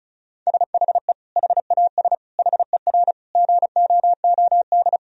Morse code, heard as a keyed tone, about 34 wpm, 700 Hz. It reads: SHE HAS HER GOOD
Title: Sentences from Top 1000 Words - Review Part 4 - 35wpm